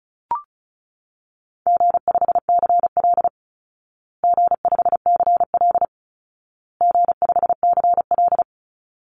Morse code, heard as a keyed tone, about 34 wpm, 700 Hz. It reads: G5CL G5CL G5CL
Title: Easy Call Signs Repeated 3x - 35wpm